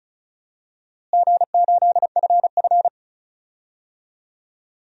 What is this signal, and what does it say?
Morse code, heard as a keyed tone, about 36 wpm, 700 Hz. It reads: G8FF
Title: DXCC-100 Call Signs - 4 Characters - Repeated 2x - 35wpm